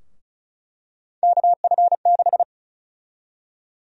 Morse code, heard as a keyed tone, about 36 wpm, 700 Hz. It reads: KF6
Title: USA Call Sign Prefixes - 35wpm